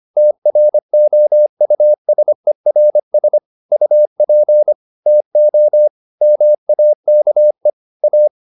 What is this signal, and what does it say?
Morse code, heard as a keyed tone, about 25 wpm, 600 Hz. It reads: TROUSERS UP TO MAKE A
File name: the old man and the sea part 1 25wpm